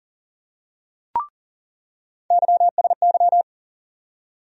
Morse code, heard as a keyed tone, about 40 wpm, 700 Hz. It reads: YSY
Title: USA Call Sign Postfixes - 3 Letters 40wpm